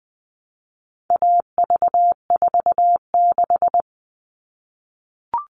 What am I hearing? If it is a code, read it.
Morse code, heard as a keyed tone, about 20 wpm, 700 Hz. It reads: AV46
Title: All Cut Numbers - 4-Digit 20wpm